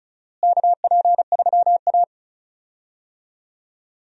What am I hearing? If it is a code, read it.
Morse code, heard as a keyed tone, about 36 wpm, 700 Hz. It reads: KP3A
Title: USA Call Signs - 4 Characters - Repeated 2x - 35wpm